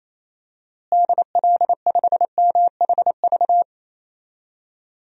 Morse code, heard as a keyed tone, about 27 wpm, 700 Hz. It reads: DL5MHV
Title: DXCC-100 Call Signs - 6 Characters - Repeated 2x - 28wpm